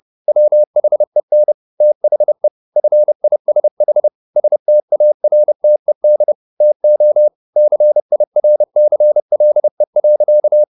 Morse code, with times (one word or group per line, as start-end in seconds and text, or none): 0.3-1.5 WHEN
1.8-2.5 THE
2.8-4.1 FISH
4.4-6.3 STARTED
6.6-7.3 TO
7.6-10.6 CIRCLE.